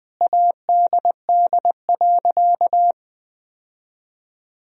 Morse code, heard as a keyed tone, about 21 wpm, 700 Hz. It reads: ADD.
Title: Sentences from Top 1000 Words - Review Part 1 - 20wpm